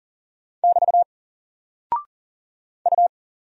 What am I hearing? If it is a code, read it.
Morse code, heard as a keyed tone, about 40 wpm, 700 Hz. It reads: <BT> U
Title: Single Character (Letter:Number:Punctuation:Prosign) Alt 40wpm